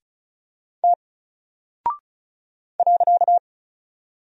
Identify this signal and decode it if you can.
Morse code, heard as a keyed tone, about 36 wpm, 700 Hz. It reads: T .
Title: Single Character (Letter:Number:Punctuation:Prosign) Alt 35wpm